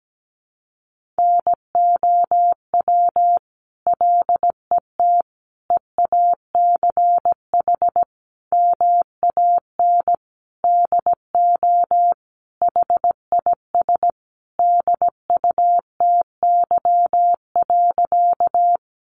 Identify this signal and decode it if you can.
Morse code, heard as a keyed tone, about 17 wpm, 700 Hz. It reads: NOW LET EACH MAN DO HIS DUTY.
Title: Sentences from Top 1500 Words 17wpm